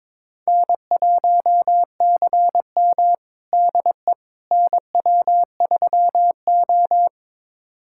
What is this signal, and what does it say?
Morse code, heard as a keyed tone, about 21 wpm, 700 Hz. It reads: N1CM DE NW3O
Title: USA Callsign DE Callsign - 4 Characters - 22wpm